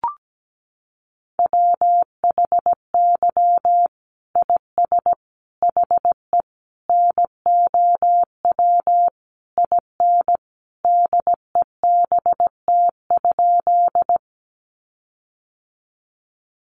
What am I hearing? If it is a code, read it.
Morse code, heard as a keyed tone, about 17 wpm, 700 Hz. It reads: WHY IS HE NOW IN DEBT?
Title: Sentences from Top 1300 Words 17wpm